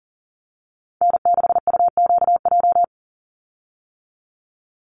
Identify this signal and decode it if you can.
Morse code, heard as a keyed tone, about 40 wpm, 700 Hz. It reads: N6UQJ
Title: USA Call Signs 40wpm